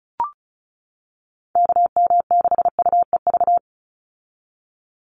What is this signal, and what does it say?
Morse code, heard as a keyed tone, about 34 wpm, 700 Hz. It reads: KM6UEV
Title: USA Call Signs 35wpm